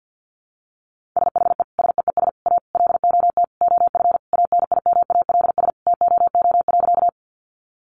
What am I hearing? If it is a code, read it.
Morse code, heard as a keyed tone, about 50 wpm, 700 Hz. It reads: SHE SEES A LOT OF ANIMALS TOO.
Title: Easy Phrases 50wpm